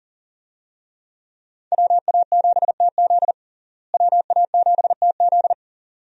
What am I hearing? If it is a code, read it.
Morse code, heard as a keyed tone, about 40 wpm, 700 Hz. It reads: WA7TZ WA7TZ
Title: USA Call Signs - 5 Characters - Repeated 2x - 40wpm